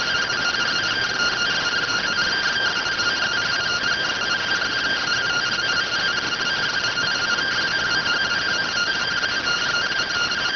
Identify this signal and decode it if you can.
Two-tone teleprinter signal, about 170 Hz shift.